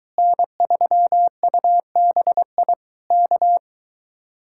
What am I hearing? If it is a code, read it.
Morse code, heard as a keyed tone, about 23 wpm, 700 Hz. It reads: N3UBI K